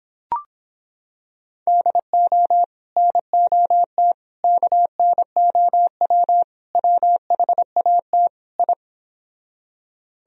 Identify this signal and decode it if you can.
Morse code, heard as a keyed tone, about 26 wpm, 700 Hz. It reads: DO NOT KNOW WHAT I